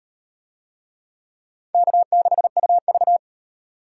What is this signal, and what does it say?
Morse code, heard as a keyed tone, about 38 wpm, 700 Hz. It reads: K6UV